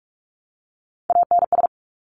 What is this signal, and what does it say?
Morse code, heard as a keyed tone, about 46 wpm, 700 Hz. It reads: ANS
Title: Top Three Letter Combinations in Words 45wpm